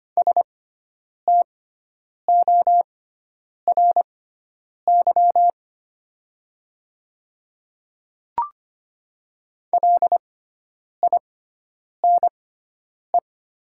Morse code, heard as a keyed tone, about 25 wpm, spaced as for 9 wpm, 700 Hz. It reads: STORY LINE
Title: 6x Character Spacing - Top 300 Words - 25wpm